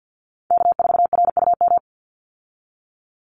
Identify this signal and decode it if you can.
Morse code, heard as a keyed tone, about 48 wpm, 700 Hz. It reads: K4RUM
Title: Call Signs Repeated 3x - 50wpm